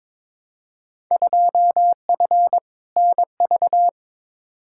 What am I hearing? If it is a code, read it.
Morse code, heard as a keyed tone, about 22 wpm, 700 Hz. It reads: 2F NV
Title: Field Day Exchanges - 22wpm